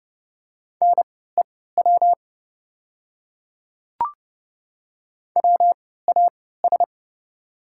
Morse code, heard as a keyed tone, about 30 wpm, spaced as for 17 wpm, 690 Hz. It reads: NEW WAS
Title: 3x Character Spacing - Top 200 Words - 30wpm